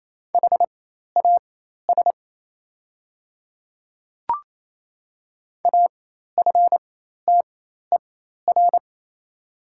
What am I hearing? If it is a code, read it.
Morse code, heard as a keyed tone, about 28 wpm, spaced as for 13 wpm, 690 Hz. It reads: HAS AFTER
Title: 4x Character Spacing - Top 300 Words - 28wpm